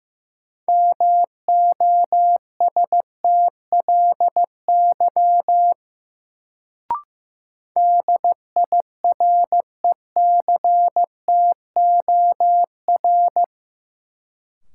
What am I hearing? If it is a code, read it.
Morse code, heard as a keyed tone, about 15 wpm, 700 Hz. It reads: MOSTLY DIRECTOR
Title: Sentences from Top 1300 Words - New Word Drill Alt - 15wpm